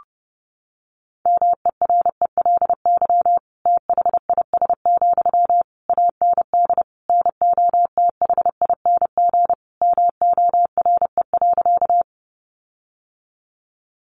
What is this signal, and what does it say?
Morse code, heard as a keyed tone, about 30 wpm, 700 Hz. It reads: MERELY THIS, AND NOTHING MORE.
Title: Sentences from Top 1400 Words 30wpm